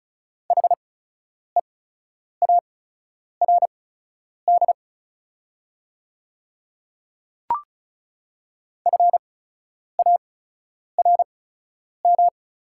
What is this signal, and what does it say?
Morse code, heard as a keyed tone, about 34 wpm, spaced as for 10 wpm, 690 Hz. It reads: HEARD FARM
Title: 8x Character Spacing - Top 500 Words - 35wpm